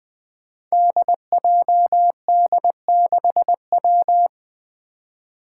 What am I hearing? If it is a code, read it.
Morse code, heard as a keyed tone, about 20 wpm, 700 Hz. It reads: DJD6W
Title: Five Challenging Letter-Number Combinations 20wpm